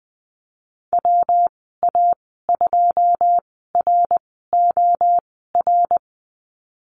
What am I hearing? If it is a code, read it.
Morse code, heard as a keyed tone, about 20 wpm, spaced as for 15 wpm, 700 Hz. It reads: WA2ROR
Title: 167 Lesson 27g - Number 7 - Call Signs 20wpm